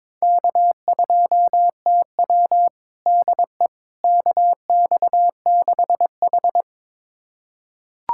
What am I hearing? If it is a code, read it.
Morse code, heard as a keyed tone, about 22 wpm, 700 Hz. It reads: K2TW DE KX6H
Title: USA Callsign DE Callsign - 4 Characters - 22wpm